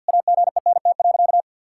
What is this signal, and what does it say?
Morse code, heard as a keyed tone, about 50 wpm, 700 Hz. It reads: AGENT.
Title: Sentences from Top 1600 Words 50wpm